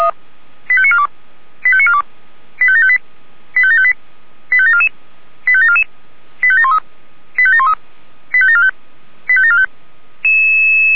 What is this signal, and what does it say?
Five-tone selective call (ZVEI), heard as quick runs of five tones, tones of about 70 ms. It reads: 85942 85942 85758 85758 85744 85744 85713 85713 85745 85745